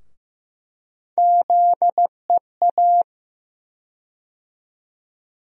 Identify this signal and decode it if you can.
Morse code, heard as a keyed tone, about 15 wpm, 700 Hz. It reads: ZEA
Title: USA Call Sign Postfixes - 3 Letters 15wpm